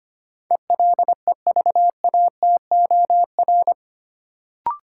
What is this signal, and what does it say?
Morse code, heard as a keyed tone, about 25 wpm, 700 Hz. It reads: ELEVATOR
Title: Sentences from Top 1600 Words - New Word Drill - 25wpm